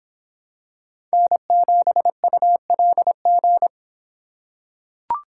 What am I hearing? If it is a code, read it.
Morse code, heard as a keyed tone, about 26 wpm, 700 Hz. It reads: N7ULG